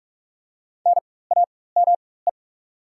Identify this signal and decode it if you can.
Morse code, heard as a keyed tone, about 44 wpm, spaced as for 21 wpm, 690 Hz. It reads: NAME
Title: 4x Character Spacing - Top 500 Words - 45wpm